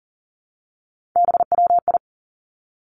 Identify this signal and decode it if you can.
Morse code, heard as a keyed tone, about 41 wpm, 700 Hz. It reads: BWI
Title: Three Challenging Letter-Number Combinations 40wpm